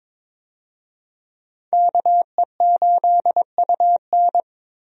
Morse code, heard as a keyed tone, about 22 wpm, 700 Hz. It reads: KE8UN
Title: USA Call Signs - 5 Characters - Repeated 2x - 22wpm